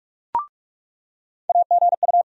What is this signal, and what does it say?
Morse code, heard as a keyed tone, about 44 wpm, 700 Hz. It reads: AGU
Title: USA Call Sign Postfixes - 3 Letters 45wpm